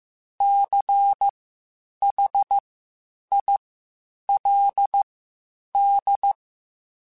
Morse code, heard as a keyed tone, about 15 wpm, spaced as for 8 wpm, 800 Hz. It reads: CHILD
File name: Quotes Spaced / Quotes Spaced-mp3s / ChildAsleep sp @13